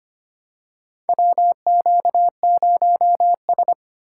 Morse code, heard as a keyed tone, about 25 wpm, 700 Hz. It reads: WQ0S
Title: USA Call Signs - 4 Characters - 3x Character Spacing - 25wpm